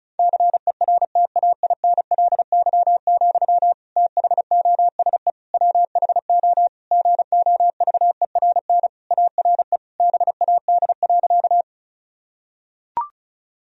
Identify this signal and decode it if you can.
Morse code, heard as a keyed tone, about 34 wpm, 700 Hz. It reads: CERTAINLY, THOSE WHO GOVERN ARE BAD.
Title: Sentences from Top 1000 Words - Review Part 2 - 35wpm